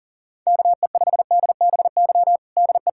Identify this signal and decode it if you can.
Morse code, heard as a keyed tone, about 40 wpm, 700 Hz. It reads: KE5DBY DE
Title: USA Callsign DE Callsign - 6 Characters - 40wpm